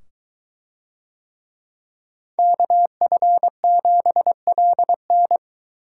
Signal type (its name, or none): Morse code